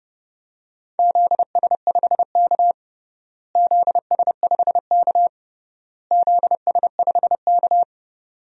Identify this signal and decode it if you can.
Morse code, heard as a keyed tone, about 29 wpm, 700 Hz. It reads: ZS5K ZS5K ZS5K
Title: DXCC-100 Call Signs - 4 Characters - Repeated 3x - 30wpm